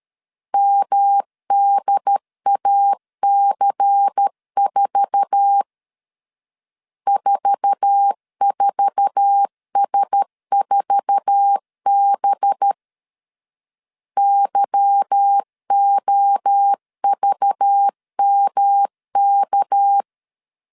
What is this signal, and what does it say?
Morse code, heard as a keyed tone, about 13 wpm, 800 Hz. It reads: MDAC4 44S4B YOVMK
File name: LEZIONI / 8 YJ45 / 8 YJ4 12wpm 2151